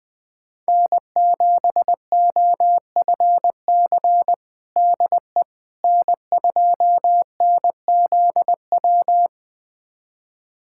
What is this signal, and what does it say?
Morse code, heard as a keyed tone, about 20 wpm, 700 Hz. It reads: N7OFC DE N2NZW
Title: USA Callsign DE Callsign - 5 Characters - 20wpm